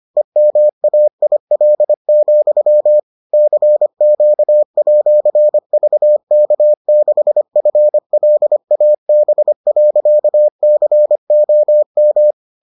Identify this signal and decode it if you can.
Morse code, heard as a keyed tone, about 25 wpm, 600 Hz. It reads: EMAIL, CQ@VK6FLAB.COM